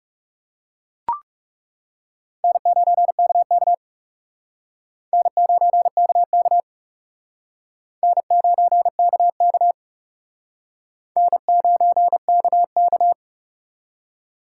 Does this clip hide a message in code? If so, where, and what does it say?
Yes: Morse code from 2.4 to 13.1 s, heard as a keyed tone, about 39 wpm, 700 Hz. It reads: N9KK N9KK N9KK N9KK